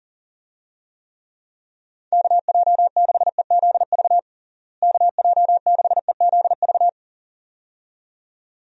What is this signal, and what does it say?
Morse code, heard as a keyed tone, about 37 wpm, 700 Hz. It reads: KJ6EZV KJ6EZV
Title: USA Call Signs - 6 Characters - Repeated 2x - 40wpm